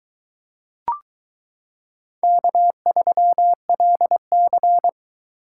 Morse code, heard as a keyed tone, about 22 wpm, 700 Hz. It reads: K3LC